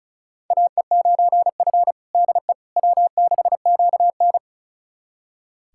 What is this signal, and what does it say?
Morse code, heard as a keyed tone, about 36 wpm, 700 Hz. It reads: AE9F DE W6QN
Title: USA Callsign DE Callsign - 4 Characters - 35wpm